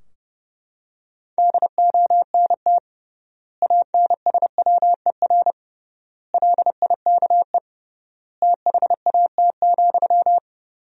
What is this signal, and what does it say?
Morse code, heard as a keyed tone, about 30 wpm, 700 Hz. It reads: DONT ANSWER LIKE THAT,
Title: 3x Word-Spacing - Sentences from Top 400 Words - 30wpm